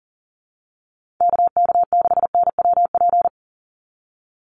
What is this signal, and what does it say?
Morse code, heard as a keyed tone, about 37 wpm, 700 Hz. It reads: KK6NWP
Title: USA Call Signs 40wpm